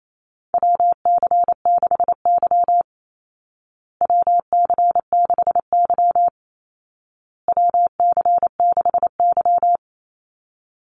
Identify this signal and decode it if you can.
Morse code, heard as a keyed tone, about 28 wpm, 700 Hz. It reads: WC6Y WC6Y WC6Y
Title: Call Signs Repeated 2x - 28wpm